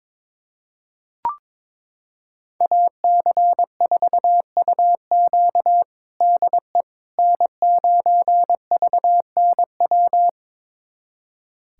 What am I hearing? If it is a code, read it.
Morse code, heard as a keyed tone, about 22 wpm, 700 Hz. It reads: AC4UQ DE N9VNW
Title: USA Callsign DE Callsign - 5 Characters - 22wpm